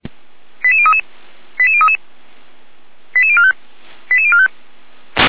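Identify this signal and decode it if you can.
Five-tone selective call (ZVEI), heard as quick runs of five tones, tones of about 70 ms. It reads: 88033 88033 88045 88045
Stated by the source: ZVEI five-tone selective call 88191 received via SDR, logged on the Feuerwehr Orlamünde fire brigade audio server.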